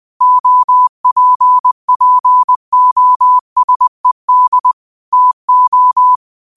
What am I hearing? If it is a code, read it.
Morse code, heard as a keyed tone, about 20 wpm, 1000 Hz. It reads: OPPOSED TO